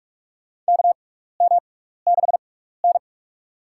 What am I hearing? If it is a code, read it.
Morse code, heard as a keyed tone, about 43 wpm, spaced as for 15 wpm, 700 Hz. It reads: KM6N
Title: USA Call Signs - 4 Characters - 6x Character Spacing - 45wpm